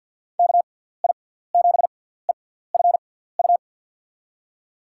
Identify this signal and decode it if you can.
Morse code, heard as a keyed tone, about 50 wpm, spaced as for 17 wpm, 700 Hz. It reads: KI7EFU
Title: USA Call Signs - 6 Characters - 6x Character Spacing - 50wpm